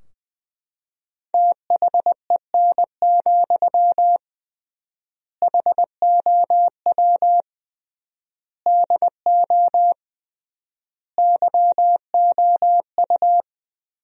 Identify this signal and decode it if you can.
Morse code, heard as a keyed tone, about 20 wpm, 700 Hz. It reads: THEN, HOW DO YOU
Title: 3x Word-Spacing - Sentences from Top 100 Words - 20wpm